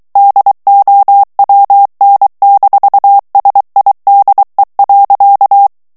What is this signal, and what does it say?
Morse code, heard as a keyed tone, about 23 wpm, 800 Hz. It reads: DOWN-SIDE.
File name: morse mp3 / ANGLAIS texte clair 1 vitesse 22 mots minute